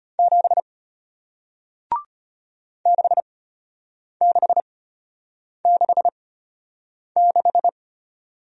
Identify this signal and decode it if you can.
Morse code, heard as a keyed tone, about 33 wpm, spaced as for 8 wpm, 700 Hz. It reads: 7 6666